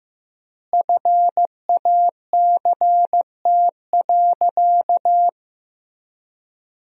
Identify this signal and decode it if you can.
Morse code, heard as a keyed tone, about 16 wpm, 700 Hz. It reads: FACT.
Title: 4x Word-Spacing - Sentences from Top 500 Words - 15wpm